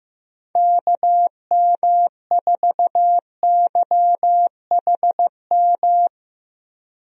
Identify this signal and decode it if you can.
Morse code, heard as a keyed tone, about 15 wpm, 700 Hz. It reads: KM4YHM
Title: USA Call Signs - 6 Characters - Repeated 3x - 15wpm